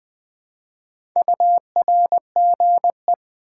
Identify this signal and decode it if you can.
Morse code, heard as a keyed tone, about 21 wpm, 700 Hz. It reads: URGE
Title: Build A Word - 1 Letter At A Time - 4-Letter Words - 20wpm